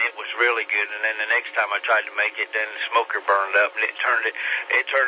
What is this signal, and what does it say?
Emergency Alert System (SAME) tones, heard as harsh buzzing bursts, with speech. It is heard from 0.3 to 4.3 s.